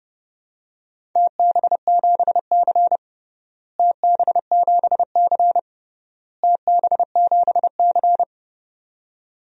Morse code, heard as a keyed tone, about 30 wpm, 700 Hz. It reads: TB7C TB7C TB7C
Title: DXCC-100 Call Signs - 4 Characters - Repeated 3x - 30wpm